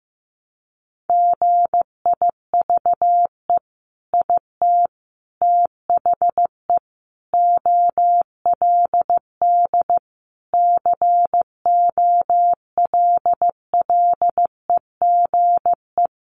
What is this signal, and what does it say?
Morse code, heard as a keyed tone, about 15 wpm, 700 Hz. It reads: GIVE IT THE OLD COLLEGE
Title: Sentences from Top 1300 Words 15wpm